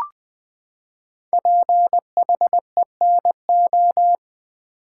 Morse code, heard as a keyed tone, about 20 wpm, 700 Hz. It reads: PHENO
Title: Build A Word - 5 Letters At A Time - 10-Letter Words - 20wpm